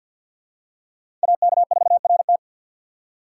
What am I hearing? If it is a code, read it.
Morse code, heard as a keyed tone, about 53 wpm, 700 Hz. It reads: AK4RT
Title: USA Call Signs - 5 Characters - 6x Character Spacing - 50wpm